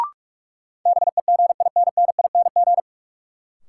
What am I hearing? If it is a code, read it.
Morse code, heard as a keyed tone, about 42 wpm, 700 Hz. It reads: BEGINNING
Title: Sentences from Top 800 Words - New Word Drill - 45wpm